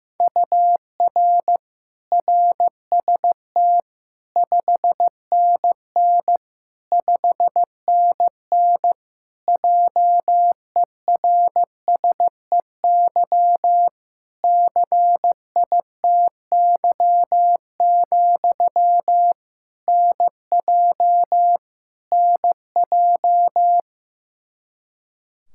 Morse code, heard as a keyed tone, about 16 wpm, 700 Hz. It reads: UR RST 5NN 5NN JERSEY CITY, NJ NJ
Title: Practice QSO - Exchange 1 - Beginning - 15wpm